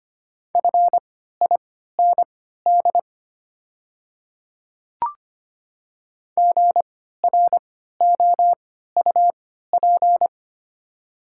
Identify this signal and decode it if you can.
Morse code, heard as a keyed tone, about 25 wpm, spaced as for 14 wpm, 690 Hz. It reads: FIND GROUP
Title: 3x Character Spacing - Top 300 Words - 25wpm